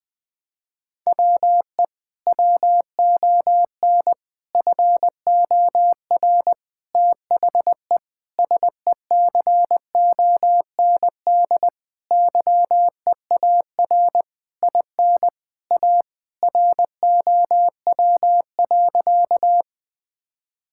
Morse code, heard as a keyed tone, about 20 wpm, 700 Hz. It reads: WE WON FOR THE SECOND YEAR IN A ROW.
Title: Sentences with 6-Word Phrases - 20wpm